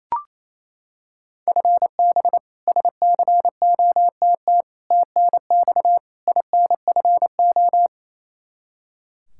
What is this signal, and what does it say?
Morse code, heard as a keyed tone, about 28 wpm, 700 Hz. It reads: FB SCOTT TNX INFO
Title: K1USN SST QSO Practice 28wpm